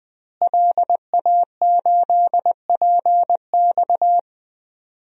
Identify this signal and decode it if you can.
Morse code, heard as a keyed tone, about 20 wpm, 700 Hz. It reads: LA8PX
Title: DXCC-100 Call Signs - 5 Characters - Repeated 3x - 20wpm